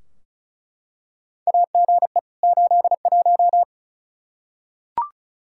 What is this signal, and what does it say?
Morse code, heard as a keyed tone, about 36 wpm, 700 Hz. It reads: AGE 81
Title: Practice QSO - Exchange 3 - More Info - 35wpm